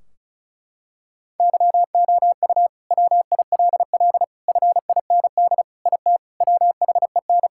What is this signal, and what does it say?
Morse code, heard as a keyed tone, about 36 wpm, 700 Hz. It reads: YOU WILL FIND IT WHEN
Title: Sentences from Top 1600 Words 35wpm